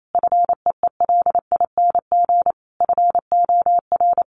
Morse code, heard as a keyed tone, about 28 wpm, 700 Hz. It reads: FEELING FOR
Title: Sentences from Top 1200 Words 28wpm